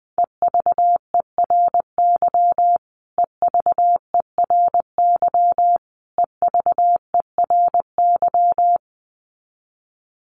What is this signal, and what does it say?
Morse code, heard as a keyed tone, about 19 wpm, 700 Hz. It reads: EVERY EVERY EVERY
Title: Top 500 Words Repeated 20wpm